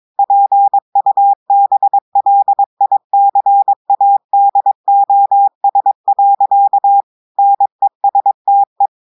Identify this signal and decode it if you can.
Morse code, heard as a keyed tone, about 22 wpm, 800 Hz. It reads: PUBLICADOS. NESTE
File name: audios / portugais / portugais texte en clair 1 vitesse 22 mots minutes